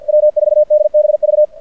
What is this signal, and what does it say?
Morse code, heard as a keyed tone, about 53 wpm, 600 Hz. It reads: W4DBV
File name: archived / CRSnet